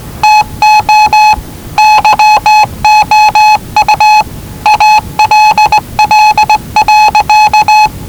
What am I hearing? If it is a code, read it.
Morse code, heard as a keyed tone, about 19 wpm, 850 Hz. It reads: TO YOU ALL.
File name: WellingtonRadioZlwCloseDownOnHfAndMf / 01-Zlw 4277khz 417.5 qrt until 0925z